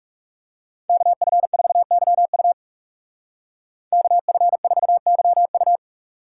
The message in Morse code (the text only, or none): KF4YU KF4YU